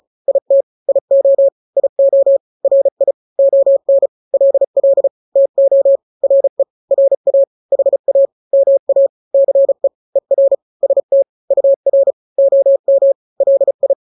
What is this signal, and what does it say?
Morse code, heard as a keyed tone, about 34 wpm, 550 Hz. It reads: IT IO IO RI ON LL TO RE RA HA MA CE ER ST UR OM LI
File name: n-grams / 2-grams 35 30 WPM